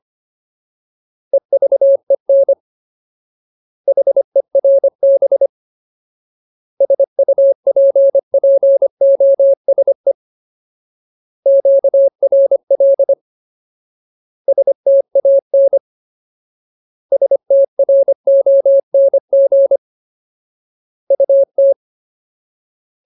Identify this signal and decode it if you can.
Morse code, heard as a keyed tone, about 25 wpm, 550 Hz. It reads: EVEN HERB SUPPOSE QRL STAN STRONG UT
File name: Ultimate Word List / 25WPM Ultimate